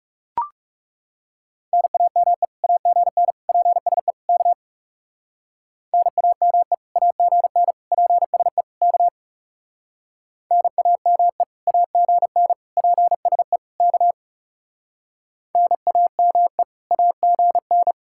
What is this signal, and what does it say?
Morse code, heard as a keyed tone, about 36 wpm, 700 Hz. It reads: NAME AGN PSE K NAME AGN PSE K NAME AGN PSE K NAME AGN
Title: Speed-Racing Common QSO Phrases Encore - 30wpm